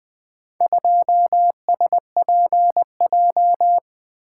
Morse code, heard as a keyed tone, about 20 wpm, 700 Hz. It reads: 2SPJ
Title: Four Letter-Number Combinations - 20wpm